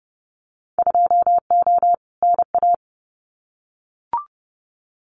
Morse code, heard as a keyed tone, about 30 wpm, 700 Hz. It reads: 2O NA
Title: Winter Field Day Exchanges 30wpm